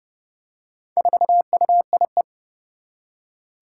Morse code, heard as a keyed tone, about 30 wpm, 700 Hz. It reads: 4UIE
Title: Four Challenging Letter-Number Combinations 30wpm